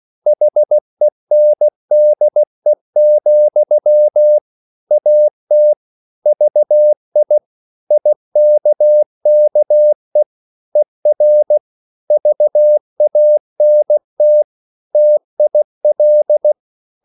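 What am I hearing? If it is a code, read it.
Morse code, heard as a keyed tone, about 16 wpm, 600 Hz. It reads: HENDE, AT VI IKKE ER VANT TIL
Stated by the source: Slow (QRS) Morse code at 16 wpm, from CW med Gnister.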